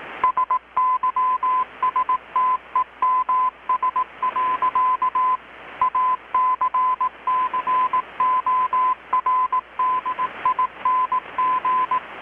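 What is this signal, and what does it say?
Morse code, heard as a keyed tone, about 18 wpm, 1020 Hz, with radio noise. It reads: SYSTEMS. ACCORDING
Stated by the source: Morse from KPH's Night of Nights XX maritime radiotelegraphy event, received in Chiapas, Mexico about 3,800 km away on a Kenwood R-600 with a grounded 10 m long-wire antenna.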